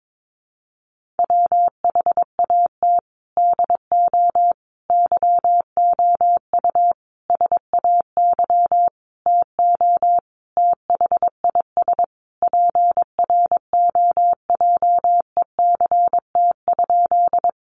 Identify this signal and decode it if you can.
Morse code, heard as a keyed tone, about 22 wpm, 700 Hz. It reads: WHAT DO YOU SAY TO THIS PROJECT?
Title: Sentences from Top 1400 Words 22wpm